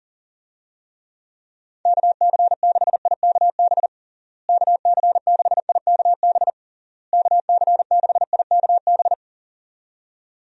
Morse code, heard as a keyed tone, about 37 wpm, 700 Hz. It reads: KC6IKB KC6IKB KC6IKB
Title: USA Call Signs - 6 Characters - Repeated 3x - 40wpm